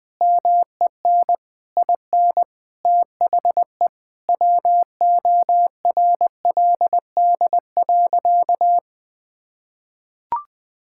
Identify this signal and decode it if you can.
Morse code, heard as a keyed tone, about 20 wpm, 700 Hz. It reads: MEN IN THE WORLD.
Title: Sentences with 8-Word Phrases - Spoken Beginning - 20wpm